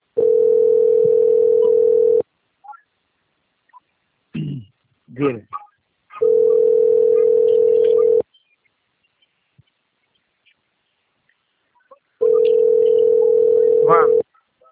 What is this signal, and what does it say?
Ringback tone (North America), 440 + 480 Hz, with speech.
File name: out- - - - - .9880